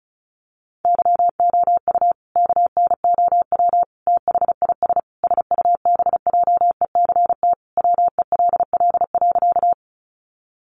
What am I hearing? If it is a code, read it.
Morse code, heard as a keyed tone, about 34 wpm, 700 Hz. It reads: YOU KNOW THIS SUBJECT WELL.
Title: Sentences from Top 800 Words 35wpm